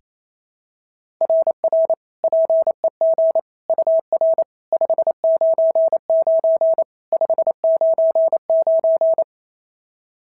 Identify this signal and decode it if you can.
Morse code, heard as a keyed tone, about 25 wpm, 650 Hz. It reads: RR PEG UR 599 599
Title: SOTA QSO Practice 28wpm